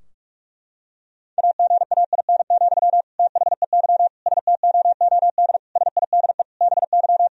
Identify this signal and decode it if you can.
Morse code, heard as a keyed tone, about 46 wpm, 700 Hz. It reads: AGAIN, THEY STOOD SIDE BY
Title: Sentences from Top 1000 Words - Review Part 2 - 45wpm